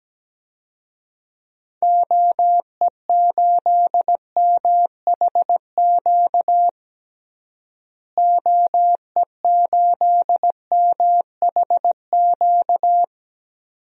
Morse code, heard as a keyed tone, about 17 wpm, 700 Hz. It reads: OE8MHQ OE8MHQ
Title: DXCC-100 Call Signs - 6 Characters - Repeated 2x - 17wpm